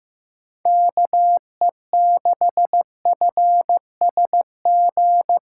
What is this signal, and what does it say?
Morse code, heard as a keyed tone, about 15 wpm, 700 Hz. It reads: KE6FSG
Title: USA Call Signs - 8x Character Spacing - 15wpm